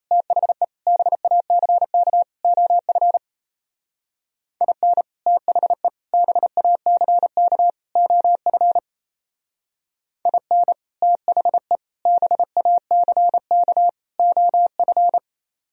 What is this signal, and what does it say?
Morse code, heard as a keyed tone, about 35 wpm, 700 Hz. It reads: THE BACK OF IN THE BACK OF IN THE BACK OF
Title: Speed-Racing - Sets of 4 Words 28wpm